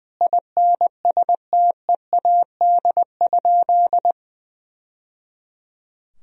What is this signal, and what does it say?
Morse code, heard as a keyed tone, about 21 wpm, 700 Hz. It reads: INSTEAD?
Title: Sentences from Top 1000 Words - Review Part 4 - 20wpm